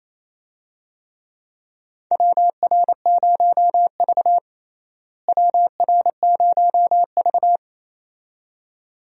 Morse code, heard as a keyed tone, about 28 wpm, 700 Hz. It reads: WR0V WR0V
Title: USA Call Signs - 4 Characters - Repeated 2x - 28wpm